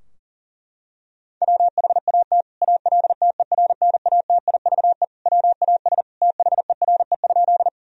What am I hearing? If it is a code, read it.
Morse code, heard as a keyed tone, about 37 wpm, 700 Hz. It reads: WHAT ALTERNATIVE WAS THERE?
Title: Sentences from Top 1000 Words - Review Part 4 - 40wpm